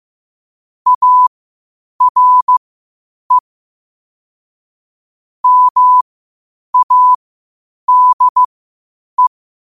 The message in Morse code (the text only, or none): ARE MADE